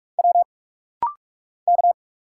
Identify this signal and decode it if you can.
Morse code, heard as a keyed tone, about 46 wpm, 700 Hz. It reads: W K
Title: Single Letters - Rapid-Fire 45wpm